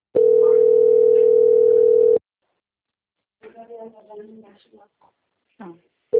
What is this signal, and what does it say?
Ringback tone (North America), 440 + 480 Hz, with speech.
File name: out- - - - - .39990